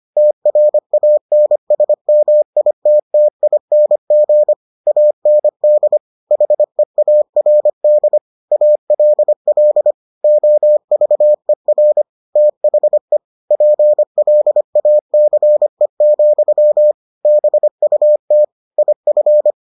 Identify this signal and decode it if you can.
Morse code, heard as a keyed tone, about 25 wpm, 600 Hz. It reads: TRANSMITTING AND HEARD ALL OVER THE PLACE, BUT IF